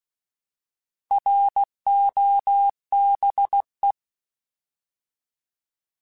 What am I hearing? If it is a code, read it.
Morse code, heard as a keyed tone, about 16 wpm, 800 Hz. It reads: ROBE